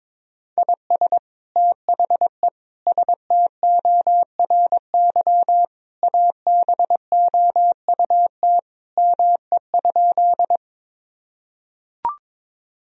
Morse code, heard as a keyed tone, about 22 wpm, 700 Hz. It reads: IS THE STORY ABOUT ME?
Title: Sentences from Top 1000 Words - Review Part 1 - 22wpm